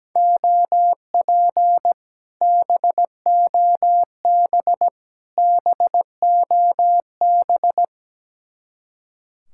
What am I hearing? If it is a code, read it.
Morse code, heard as a keyed tone, about 17 wpm, 700 Hz. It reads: OP BOB BOB
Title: Speed-Racing Common QSO Phrases - 17wpm